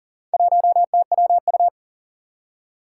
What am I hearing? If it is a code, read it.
Morse code, heard as a keyed tone, about 39 wpm, 700 Hz. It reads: 1TWU
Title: Four Challenging Letter-Number Combinations 40wpm